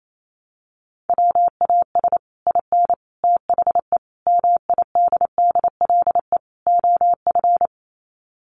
Morse code, heard as a keyed tone, about 28 wpm, 690 Hz. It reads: WAS IN THE MIDDLE OF
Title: Sets of 5 Words Encore 28wpm